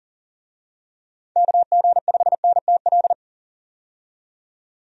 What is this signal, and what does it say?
Morse code, heard as a keyed tone, about 40 wpm, 700 Hz. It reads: KG5NTL
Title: USA Call Signs - 6 Characters - 40wpm